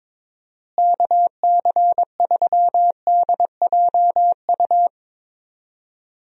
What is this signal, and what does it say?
Morse code, heard as a keyed tone, about 21 wpm, 700 Hz. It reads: KC3DJU
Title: USA Call Signs - 6 Characters - Repeated 3x - 22wpm